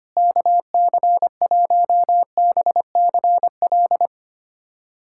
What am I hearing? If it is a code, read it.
Morse code, heard as a keyed tone, about 25 wpm, 700 Hz. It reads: KC1BCL
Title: USA Callsign DE Callsign 25wpm